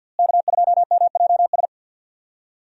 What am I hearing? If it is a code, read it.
Morse code, heard as a keyed tone, about 52 wpm, 700 Hz. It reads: K2MJS
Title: USA Call Signs - 5 Characters - Repeated 3x - 50wpm